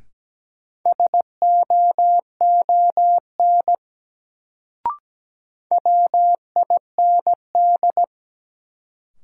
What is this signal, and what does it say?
Morse code, heard as a keyed tone, about 17 wpm, 700 Hz. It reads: SOON WIND
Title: Sentences from Top 400 Words - New Word Drill - 17wpm